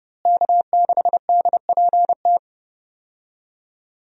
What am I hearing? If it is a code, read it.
Morse code, heard as a keyed tone, about 30 wpm, 700 Hz. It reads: K6DPT
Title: USA Call Signs - 5 Characters - Repeated 2x - 30wpm